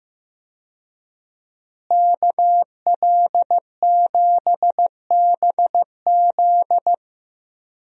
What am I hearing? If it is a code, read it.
Morse code, heard as a keyed tone, about 15 wpm, 700 Hz. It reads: KL7BZ
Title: DXCC-100 Call Signs - 5 Characters - Repeated 2x - 15wpm